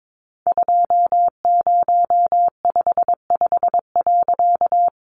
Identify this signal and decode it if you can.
Morse code, heard as a keyed tone, about 22 wpm, 700 Hz. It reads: 2055.
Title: A Sound Of Thunder 22wpm